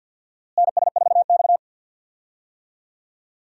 Morse code, heard as a keyed tone, about 53 wpm, 700 Hz. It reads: NS4X
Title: USA Call Signs - 4 Characters - Repeated 3x - 50wpm